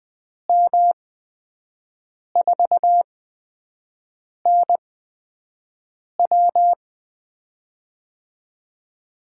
Morse code, heard as a keyed tone, about 20 wpm, spaced as for 5 wpm, 700 Hz. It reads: M4NW
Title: USA Call Signs - 5 Characters - 8x Character Spacing - 20wpm